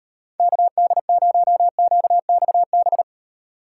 Morse code, heard as a keyed tone, about 38 wpm, 700 Hz. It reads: KD0QXB